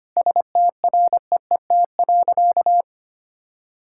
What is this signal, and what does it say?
Morse code, heard as a keyed tone, about 25 wpm, 690 Hz. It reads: STREET.
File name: practice / 25wpm / Sentences with 2-Word Phrases - 25wpm